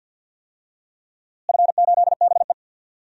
Morse code, heard as a keyed tone, about 48 wpm, 700 Hz. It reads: F8BE
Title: DXCC-100 Call Signs - 4 Characters - Repeated 2x - 50wpm